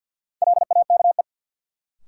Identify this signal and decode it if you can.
Morse code, heard as a keyed tone, about 51 wpm, 700 Hz. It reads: LAKE
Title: Sentences from Top 1000 Words - Review Part 3 - New Word Drill - 50wpm